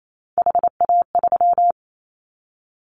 Morse code, heard as a keyed tone, about 28 wpm, 700 Hz. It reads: HA3
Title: Callsign Prefixes for Top 50 DXCC - 28wpm